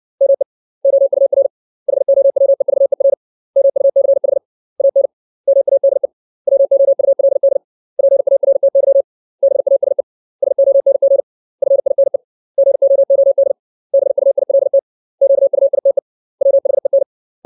Morse code, heard as a keyed tone, about 60 wpm, 550 Hz. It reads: ME OUR HOWEVER MUCH AN MADE WOULD PARTY BASE SONG FINE GOOD BUILT QUITE WHEN
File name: 632 words / 632 Words 60WPM